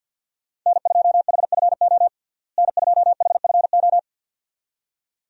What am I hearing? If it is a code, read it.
Morse code, heard as a keyed tone, about 47 wpm, 700 Hz. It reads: N2HFO N2HFO
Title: USA Call Signs - 5 Characters - Repeated 2x - 50wpm